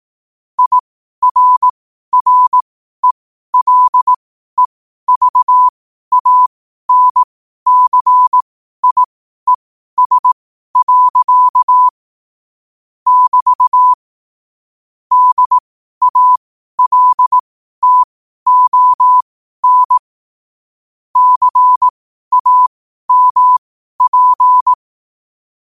Morse code, heard as a keyed tone, about 18 wpm, spaced as for 12 wpm, 1000 Hz. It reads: IRRELEVANCIES. <BT> DALTON CAMP
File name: qotdfiles / 12wpm